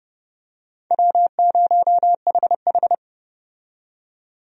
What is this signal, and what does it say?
Morse code, heard as a keyed tone, about 30 wpm, 700 Hz. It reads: W0HH
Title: USA Call Signs - 4 Characters - 30wpm